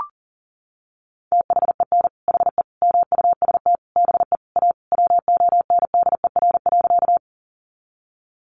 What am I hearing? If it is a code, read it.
Morse code, heard as a keyed tone, about 40 wpm, 700 Hz. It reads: THEN HE MUST BE A WONDER.
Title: Sentences from Top 500 Words 40wpm